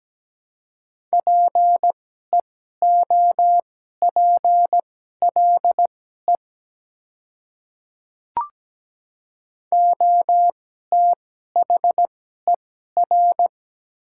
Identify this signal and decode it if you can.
Morse code, heard as a keyed tone, about 17 wpm, spaced as for 12 wpm, 700 Hz. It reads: PEOPLE OTHER
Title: 2x Character Spacing - Top 200 Words - 17wpm